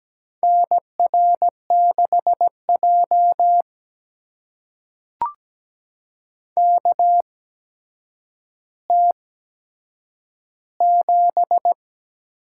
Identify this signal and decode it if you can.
Morse code, heard as a keyed tone, about 17 wpm, spaced as for 8 wpm, 700 Hz. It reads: NR6J KT7
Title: USA Call Signs - 4 Characters - 8x Character Spacing - 17wpm